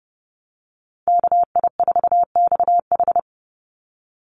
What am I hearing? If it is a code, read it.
Morse code, heard as a keyed tone, about 30 wpm, 700 Hz. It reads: KI4XH
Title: Call Signs Repeated 3x - 30wpm